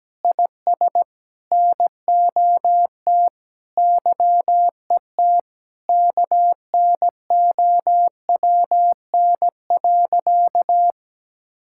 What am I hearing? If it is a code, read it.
Morse code, heard as a keyed tone, about 17 wpm, 700 Hz. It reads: IS NOT YET KNOWN.
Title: Sentences from Top 1000 Words - Review Part 5 - 17wpm